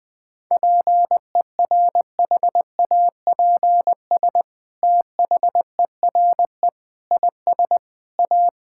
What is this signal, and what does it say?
Morse code, heard as a keyed tone, about 20 wpm, 700 Hz. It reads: PERHAPS THERE IS A
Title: Sentences from Top 1000 Words - Review Part 5 - 20wpm